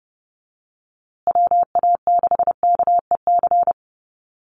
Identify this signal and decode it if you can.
Morse code, heard as a keyed tone, about 30 wpm, 700 Hz. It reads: WA6KEC
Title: USA Call Signs Repeated 2x 30wpm